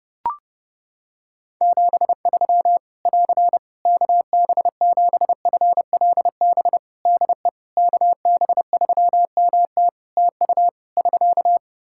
Morse code, heard as a keyed tone, about 32 wpm, 700 Hz. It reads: 73 <AR> KB7FLB DE KB3MT TU <SK>
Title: Practice QSO - Exchange 4 - Ending - 30wpm